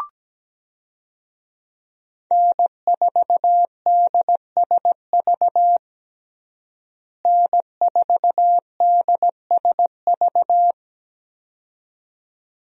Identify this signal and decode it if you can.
Morse code, heard as a keyed tone, about 17 wpm, 700 Hz. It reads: N4DSV N4DSV
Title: USA Call Signs - 5 Characters - Repeated 2x - 17wpm